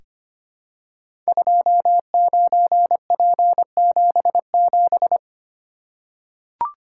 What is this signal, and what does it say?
Morse code, heard as a keyed tone, about 25 wpm, 700 Hz. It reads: 29P77
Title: Code Groups - 5 Letter:Numbers - 25wpm